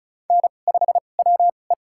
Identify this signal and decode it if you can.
Morse code, heard as a keyed tone, about 36 wpm, spaced as for 25 wpm, 700 Hz. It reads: N5WE
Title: USA Call Signs - 4 Characters - 2x Character Spacing - 35wpm